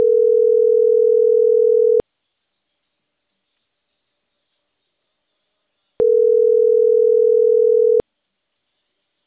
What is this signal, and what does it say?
Ringback tone (North America), 440 + 480 Hz.